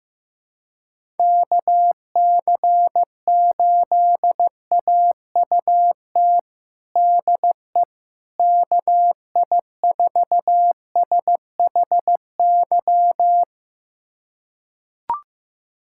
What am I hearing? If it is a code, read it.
Morse code, heard as a keyed tone, about 15 wpm, 700 Hz. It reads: KC8AUT DE KI4SHY
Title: USA Callsign DE Callsign - 6 Characters - 15wpm